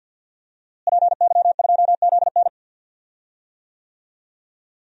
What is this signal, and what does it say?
Morse code, heard as a keyed tone, about 46 wpm, 700 Hz. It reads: PY2ZN
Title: DXCC-100 Call Signs - 5 Characters - Repeated 3x - 50wpm